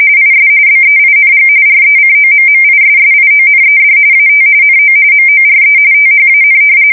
RTTY, about 45.45 baud. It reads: IS BEING RECOMMISSIONED AT NO MORE THAN
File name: fsk170 / battleshipnj